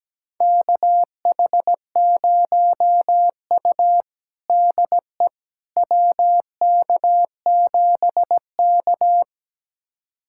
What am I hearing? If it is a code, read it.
Morse code, heard as a keyed tone, about 17 wpm, 700 Hz. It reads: KH0U DE WK7K
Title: USA Callsign DE Callsign - 4 Characters - 17wpm